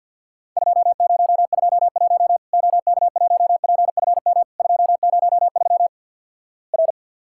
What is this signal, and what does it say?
Morse code, heard as a keyed tone, about 50 wpm, 700 Hz. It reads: 2021 OK1PFM 203 R
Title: Worked All Europe DX Contest - QTC Exchange with Repeats - 50wpm